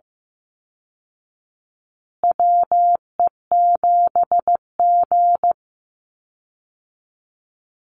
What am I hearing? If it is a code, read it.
Morse code, heard as a keyed tone, about 15 wpm, 700 Hz. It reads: WE7G
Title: Call Signs Repeated 3x - 15wpm